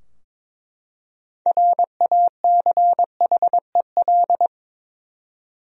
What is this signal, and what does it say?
Morse code, heard as a keyed tone, about 21 wpm, 700 Hz. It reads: RACHEL
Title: USA Common Names - 22wpm